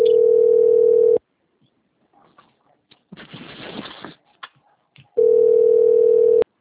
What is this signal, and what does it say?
Ringback tone (North America), 440 + 480 Hz.